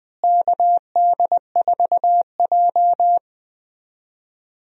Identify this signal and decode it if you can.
Morse code, heard as a keyed tone, about 20 wpm, 700 Hz. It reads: KD4J
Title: USA Call Signs - 4 Characters - Repeated 2x - 20wpm